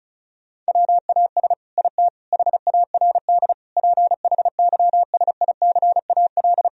Morse code, heard as a keyed tone, about 34 wpm, 700 Hz. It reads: WAS IT HARD PHYSICAL
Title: Sentences from Top 1000 Words - Review Part 4 - 35wpm